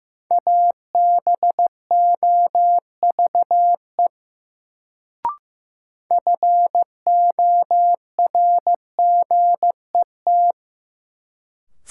Morse code, heard as a keyed tone, about 15 wpm, 700 Hz. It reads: ABOVE FORGET
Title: Sentences from Top 1200 Words - New Word Drill Alt - 15wpm